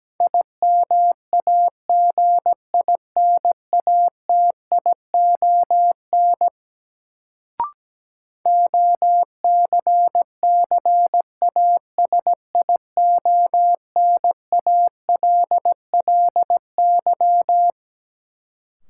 Morse code, heard as a keyed tone, about 17 wpm, 700 Hz. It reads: IMAGINATION OCCASIONALLY
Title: Sentences from Top 1300 Words - New Word Drill Alt - 17wpm